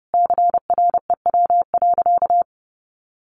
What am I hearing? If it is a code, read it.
Morse code, heard as a keyed tone, about 30 wpm, 700 Hz. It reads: CREW.
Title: Sentences from Top 1500 Words 30wpm